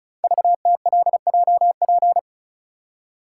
Morse code, heard as a keyed tone, about 36 wpm, 690 Hz. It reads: VTLJP
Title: Code Groups - 5 Letters - 35wpm